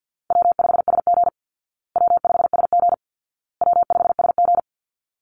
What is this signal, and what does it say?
Morse code, heard as a keyed tone, about 50 wpm, 700 Hz. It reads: W5SG W5SG W5SG
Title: Call Signs Repeated 2x - 50wpm